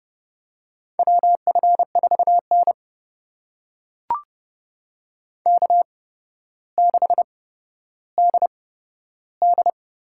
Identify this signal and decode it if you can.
Morse code, heard as a keyed tone, about 30 wpm, spaced as for 13 wpm, 700 Hz. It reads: WF4N K6DD
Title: USA Call Signs - 4 Characters - 8x Character Spacing - 30wpm